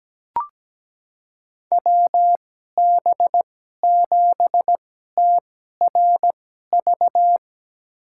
Morse code, heard as a keyed tone, about 17 wpm, spaced as for 12 wpm, 700 Hz. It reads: WB7TRV
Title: USA Call Signs - 2x Character Spacing - 17wpm